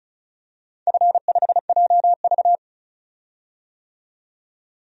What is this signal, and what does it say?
Morse code, heard as a keyed tone, about 34 wpm, 700 Hz. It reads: F5JV
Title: DXCC-100 Call Signs - 4 Characters - 35wpm